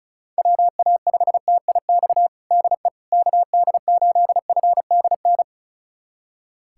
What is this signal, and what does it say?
Morse code, heard as a keyed tone, about 36 wpm, 700 Hz. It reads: WA5TIX DE KD8FDN
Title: USA Callsign DE Callsign - 6 Characters - 35wpm